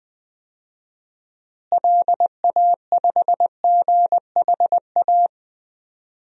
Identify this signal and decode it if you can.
Morse code, heard as a keyed tone, about 21 wpm, 700 Hz. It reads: LA5GHA
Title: DXCC-100 Call Signs - 6 Characters - Repeated 3x - 20wpm